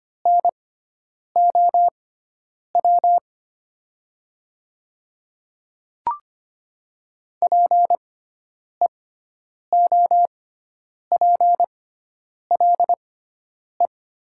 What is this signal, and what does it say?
Morse code, heard as a keyed tone, about 25 wpm, spaced as for 9 wpm, 700 Hz. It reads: NOW PEOPLE
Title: 6x Character Spacing - Top 100 Words - 25wpm